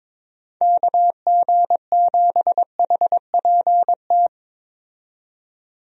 Morse code, heard as a keyed tone, about 22 wpm, 700 Hz. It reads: KG7HPT
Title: USA Call Signs - 6 Characters - 22wpm